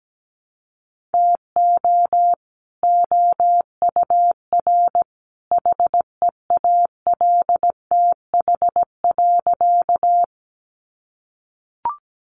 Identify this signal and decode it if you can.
Morse code, heard as a keyed tone, about 17 wpm, 700 Hz. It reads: TO OUR HEALTH.
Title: Sentences from Top 600 Words 17wpm